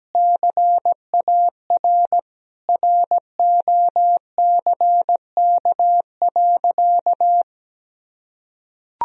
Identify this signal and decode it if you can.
Morse code, heard as a keyed tone, about 17 wpm, 700 Hz. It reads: CAR ROCK.
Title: Sentences from Top 1000 Words - Review Part 1 - 17wpm